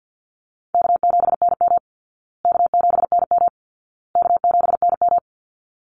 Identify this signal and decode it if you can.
Morse code, heard as a keyed tone, about 50 wpm, 700 Hz. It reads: K7NM K7NM K7NM
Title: Call Signs Repeated 2x - 50wpm